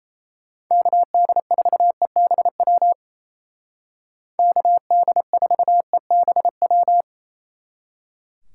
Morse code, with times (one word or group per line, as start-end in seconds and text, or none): 0.7-2.9 KD4EBW
4.4-7.0 KD4EBW